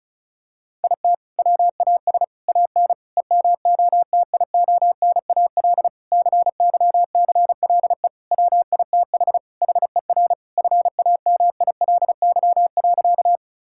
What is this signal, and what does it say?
Morse code, heard as a keyed tone, about 36 wpm, 690 Hz. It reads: IT WAS AN EMOTIONAL CYCLE WITH HER FAMILY.
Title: Sentences from Top 1000 Words - Review Part 6 - 35wpm